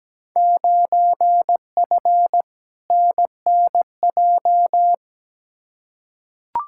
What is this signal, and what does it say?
Morse code, heard as a keyed tone, about 17 wpm, 700 Hz. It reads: 9F NNJ
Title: Field Day Exchanges Encore - 17wpm